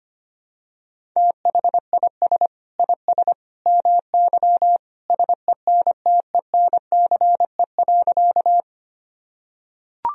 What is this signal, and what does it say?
Morse code, heard as a keyed tone, about 25 wpm, 700 Hz. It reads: THIS IS MY SENTENCE.
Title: Sentences from Top 1000 Words - Review Part 1 - 25wpm